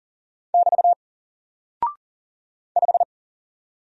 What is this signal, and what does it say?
Morse code, heard as a keyed tone, about 40 wpm, 700 Hz. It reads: <BT> 5
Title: Single Character (Letter:Number:Punctuation:Prosign) Alt 40wpm